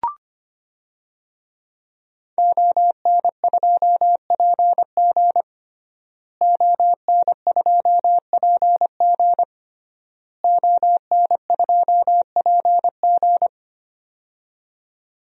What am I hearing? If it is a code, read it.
Morse code, heard as a keyed tone, about 25 wpm, 700 Hz. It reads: ON2PG ON2PG ON2PG
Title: DXCC-100 Call Signs - 5 Characters - Repeated 3x - 25wpm